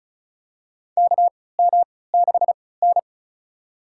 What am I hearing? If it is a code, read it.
Morse code, heard as a keyed tone, about 36 wpm, spaced as for 20 wpm, 700 Hz. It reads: KM6N
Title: USA Call Signs - 4 Characters - 3x Character Spacing - 35wpm